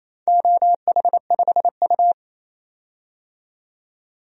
Morse code, heard as a keyed tone, about 28 wpm, 700 Hz. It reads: OH5U
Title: DXCC-100 Call Signs - 4 Characters - 28wpm